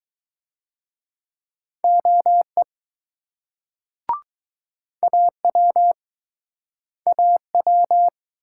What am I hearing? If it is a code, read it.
Morse code, heard as a keyed tone, about 23 wpm, 700 Hz. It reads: OE AW AW